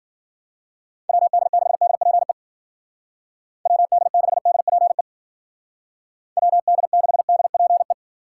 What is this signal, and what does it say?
Morse code, heard as a keyed tone, about 50 wpm, 700 Hz. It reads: WD6DPE WD6DPE WD6DPE
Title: Speed Racing - USA 2x3 Call Signs 40wpm